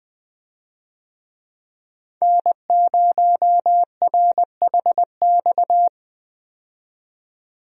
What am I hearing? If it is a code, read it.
Morse code, heard as a keyed tone, about 20 wpm, 700 Hz. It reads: N0RHX
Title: USA Call Signs - 5 Characters - 20wpm